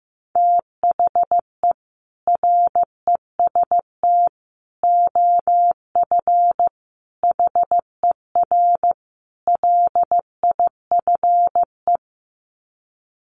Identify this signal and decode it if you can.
Morse code, heard as a keyed tone, about 15 wpm, 700 Hz. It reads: THE REST OF HER LIFE
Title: Sets of 5 Words Encore 15wpm